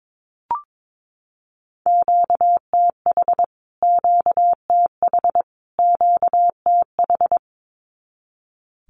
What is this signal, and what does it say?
Morse code, heard as a keyed tone, about 22 wpm, 700 Hz. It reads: QTH QTH QTH
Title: Q-Codes Repeated 3x 22wpm